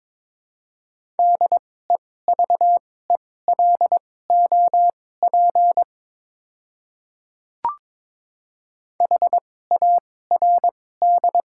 Morse code, heard as a keyed tone, about 22 wpm, spaced as for 16 wpm, 700 Hz. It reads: DEVELOP HARD
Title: 2x Character Spacing - Top 500 Words - 22wpm